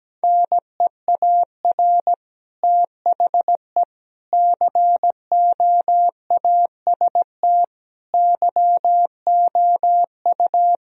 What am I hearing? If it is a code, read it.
Morse code, heard as a keyed tone, about 17 wpm, 700 Hz. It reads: NEAR THE COAST YOU
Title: Sentences from Top 1000 Words - Review Part 4 - 17wpm